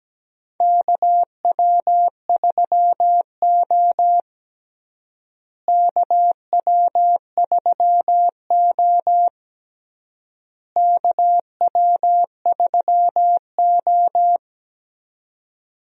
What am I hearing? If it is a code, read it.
Morse code, heard as a keyed tone, about 17 wpm, 700 Hz. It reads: KW3O KW3O KW3O
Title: USA Call Signs - 4 Characters - Repeated 3x - 17wpm